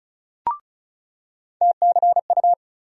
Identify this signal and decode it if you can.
Morse code, heard as a keyed tone, about 36 wpm, 700 Hz. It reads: TCU
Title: USA Call Sign Postfixes - 3 Letters 35wpm